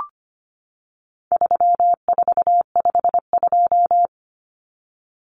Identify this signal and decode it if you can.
Morse code, heard as a keyed tone, about 25 wpm, 700 Hz. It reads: 3452
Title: Numbers - 4-Digit 25wpm